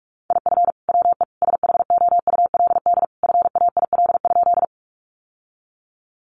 Morse code, heard as a keyed tone, about 44 wpm, 700 Hz. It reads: IF WE SHOULD FAIL?
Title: Sentences from Top 1100 Words 45wpm